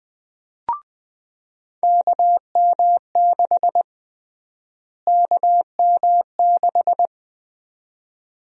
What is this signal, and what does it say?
Morse code, heard as a keyed tone, about 20 wpm, 700 Hz. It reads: KM6 KM6
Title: USA Call Sign Prefixes - Repeated 2x - 20wpm